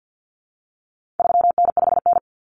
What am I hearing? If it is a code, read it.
Morse code, heard as a keyed tone, about 50 wpm, 700 Hz. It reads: 3N5N
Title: Common Cut Numbers - 4-Digit 50wpm